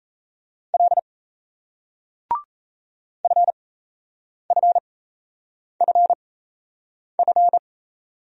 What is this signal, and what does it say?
Morse code, heard as a keyed tone, about 40 wpm, spaced as for 8 wpm, 700 Hz. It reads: L FFFF